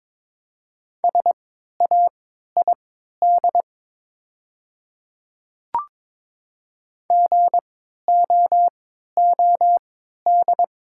Morse code, heard as a keyed tone, about 22 wpm, spaced as for 12 wpm, 700 Hz. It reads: SAID GOOD
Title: 3x Character Spacing - Top 200 Words - 22wpm